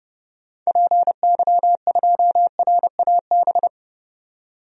Morse code, heard as a keyed tone, about 30 wpm, 700 Hz. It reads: PY2RAB
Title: DXCC-100 Call Signs - 6 Characters - 30wpm